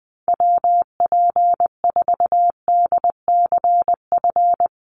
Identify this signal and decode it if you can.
Morse code, heard as a keyed tone, about 20 wpm, 700 Hz. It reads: WP4DCF
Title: USA Call Signs Repeated 2x 20wpm